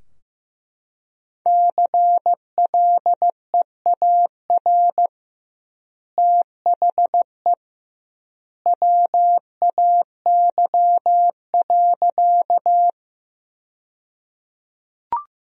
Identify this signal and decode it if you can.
Morse code, heard as a keyed tone, about 15 wpm, 700 Hz. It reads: CLEAR THE WAY.
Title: 2x Word-Spacing - Sentences from Top 500 Words - 15wpm